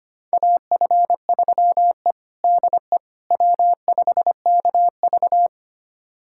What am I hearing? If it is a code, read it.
Morse code, heard as a keyed tone, about 25 wpm, 700 Hz. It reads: AF3E DE W5KV
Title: USA Callsign DE Callsign - 4 Characters - 25wpm